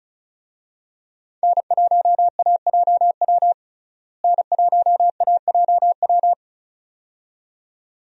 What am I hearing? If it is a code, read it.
Morse code, heard as a keyed tone, about 34 wpm, 700 Hz. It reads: N1AJW N1AJW
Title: USA Call Signs - 5 Characters - Repeated 2x - 35wpm